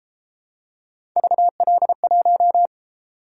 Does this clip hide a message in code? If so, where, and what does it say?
Yes: Morse code from 1.2 to 2.7 s, heard as a keyed tone, about 30 wpm, 700 Hz. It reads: VL1